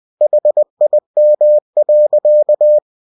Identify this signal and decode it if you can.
Morse code, heard as a keyed tone, about 20 wpm, 600 Hz. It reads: HIM.
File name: cw-books / winnie-the-pooh / winnie-20wpm-page-0100